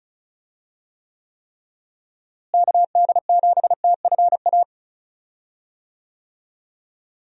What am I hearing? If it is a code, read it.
Morse code, heard as a keyed tone, about 36 wpm, 700 Hz. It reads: KD7TFA
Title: USA Call Signs - 6 Characters - Repeated 3x - 35wpm